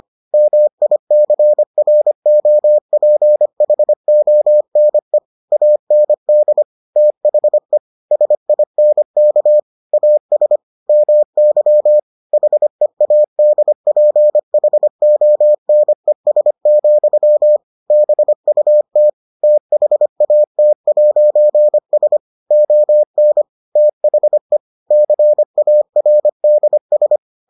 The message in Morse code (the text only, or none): MICROPHONE AND THE SINK AS MY HEADPHONES, BUT THAT'S ON THE CARDS